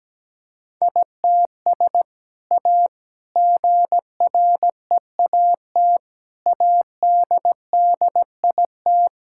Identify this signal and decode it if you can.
Morse code, heard as a keyed tone, about 17 wpm, 700 Hz. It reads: ITS A GREAT ADDIT
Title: Sentences from Top 1000 Words - Review Part 6 - 17wpm